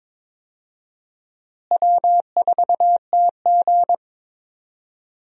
Morse code, heard as a keyed tone, about 22 wpm, 700 Hz. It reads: W4TG
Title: USA Call Signs - 4 Characters - Repeated 3x - 22wpm